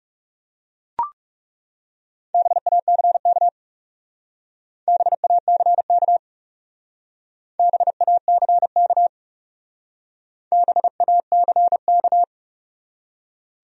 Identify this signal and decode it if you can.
Morse code, heard as a keyed tone, about 40 wpm, 700 Hz. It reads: BACK BACK BACK BACK